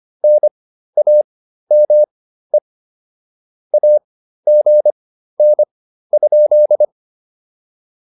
Morse code, heard as a keyed tone, about 25 wpm, spaced as for 13 wpm, 600 Hz. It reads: NAME AGN?